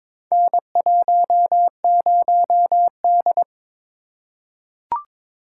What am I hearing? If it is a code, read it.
Morse code, heard as a keyed tone, about 23 wpm, 700 Hz. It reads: N10D
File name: practice / 22wpm / Four Letter-Number Combinations - 22wpm